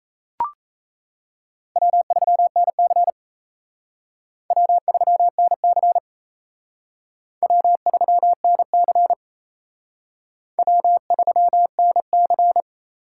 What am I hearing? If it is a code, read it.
Morse code, heard as a keyed tone, about 35 wpm, 700 Hz. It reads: W3NC W3NC W3NC W3NC
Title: Speed Racing - USA Call Signs 28wpm